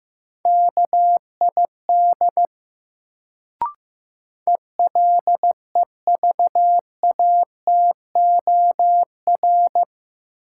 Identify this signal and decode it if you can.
Morse code, heard as a keyed tone, about 15 wpm, 700 Hz. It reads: KID ELEVATOR
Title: Sentences from Top 1600 Words - New Word Drill - 15wpm